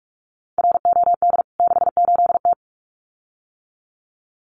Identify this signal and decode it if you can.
Morse code, heard as a keyed tone, about 44 wpm, 700 Hz. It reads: ROD 68T
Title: CWT Exchange 45wpm